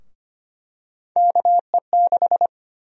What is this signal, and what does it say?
Morse code, heard as a keyed tone, about 25 wpm, 700 Hz. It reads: KE6
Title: USA Call Sign Prefixes - Repeated 2x - 25wpm